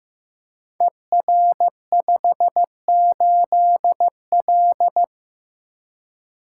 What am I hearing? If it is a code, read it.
Morse code, heard as a keyed tone, about 15 wpm, 700 Hz. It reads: ER58L
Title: Code Groups - 5 Letter:Numbers - 15wpm